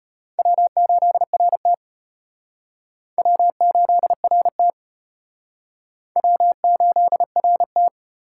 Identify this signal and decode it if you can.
Morse code, heard as a keyed tone, about 33 wpm, 700 Hz. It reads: W8RT W8RT W8RT